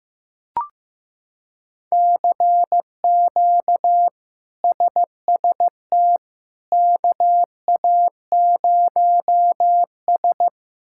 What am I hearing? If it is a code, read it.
Morse code, heard as a keyed tone, about 16 wpm, 700 Hz. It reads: CQ SST KA0S
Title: K1USN SST QSO Practice - 15wpm